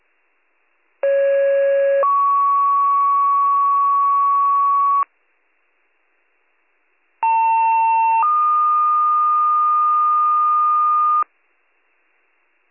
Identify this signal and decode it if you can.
Two-tone page, heard as a one-second tone, then a three-second tone, about 550 then 1120 Hz, about 900 then 1220 Hz. It is heard from 1.0 to 11.2 s.